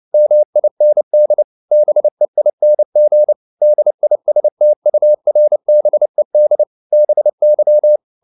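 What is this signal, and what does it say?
Morse code, heard as a keyed tone, about 29 wpm, 600 Hz. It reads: MIND BEING DISTURBED BY
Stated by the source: Morse code practice at 28 wpm, text of Ken Follett's novel Never in English, from CW med Gnister.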